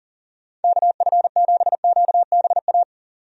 Morse code, heard as a keyed tone, about 37 wpm, 700 Hz. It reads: KF7QBA
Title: USA Call Signs - 6 Characters - 40wpm